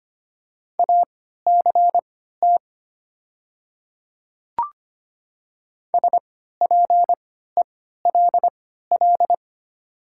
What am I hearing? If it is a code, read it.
Morse code, heard as a keyed tone, about 25 wpm, spaced as for 14 wpm, 700 Hz. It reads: ACT SPELL
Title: 3x Character Spacing - Top 500 Words - 25wpm